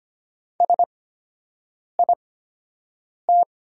Morse code, heard as a keyed tone, about 25 wpm, spaced as for 7 wpm, 690 Hz. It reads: SIT
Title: 8x Character Spacing - Top 500 Words - 25wpm